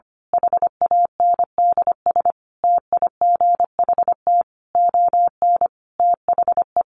Morse code, heard as a keyed tone, about 25 wpm, 700 Hz. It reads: HANDS TIGHT ON THE
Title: A Sound Of Thunder 25wpm